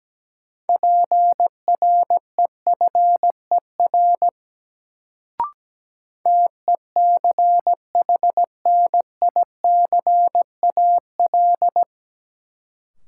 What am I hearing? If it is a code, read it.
Morse code, heard as a keyed tone, about 17 wpm, 700 Hz. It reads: PREFER TECHNICAL
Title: Sentences from Top 1100 Words - New Word Drill Alt - 17wpm